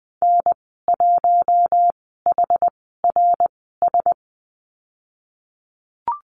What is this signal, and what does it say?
Morse code, heard as a keyed tone, about 20 wpm, spaced as for 15 wpm, 700 Hz. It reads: N1HRS
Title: 076 Lesson 13g - Letter L - Call Signs 20wpm